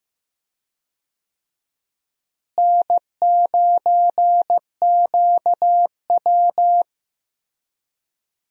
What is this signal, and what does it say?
Morse code, heard as a keyed tone, about 16 wpm, 700 Hz. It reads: N9QW
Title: USA Call Signs - 4 Characters - Repeated 2x - 15wpm